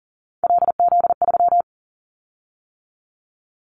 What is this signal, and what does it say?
Morse code, heard as a keyed tone, about 40 wpm, 700 Hz. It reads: LZ3
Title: Callsign Prefixes for Top 100 DXCC - 40wpm